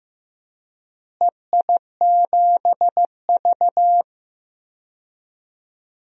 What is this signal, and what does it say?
Morse code, heard as a keyed tone, about 15 wpm, 700 Hz. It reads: EI7V
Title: DXCC-100 Call Signs - 4 Characters - Repeated 3x - 15wpm